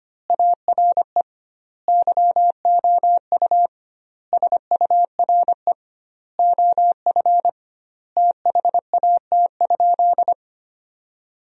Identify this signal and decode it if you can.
Morse code, heard as a keyed tone, about 25 wpm, 700 Hz. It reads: ARE YOU SURE OF THAT?
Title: 2x Word-Spacing - Sentences from Top 300 Words - 25wpm